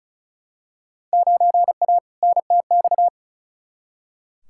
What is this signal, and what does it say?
Morse code, heard as a keyed tone, about 36 wpm, 700 Hz. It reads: 9A NTX
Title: Field Day Exchanges Encore - 35wpm